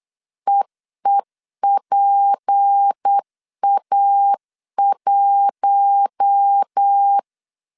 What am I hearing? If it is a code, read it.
Morse code, heard as a keyed tone, about 8 wpm, 800 Hz. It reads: EEPA1
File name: CW / DET / 1 6 1500C S A 8wpm 5203